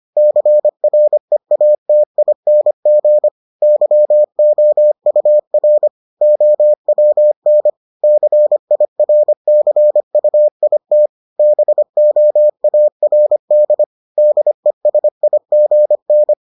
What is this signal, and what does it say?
Morse code, heard as a keyed tone, about 25 wpm, 600 Hz. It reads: CREATING YOUR OWN CIRCUIT BOARD DESIGN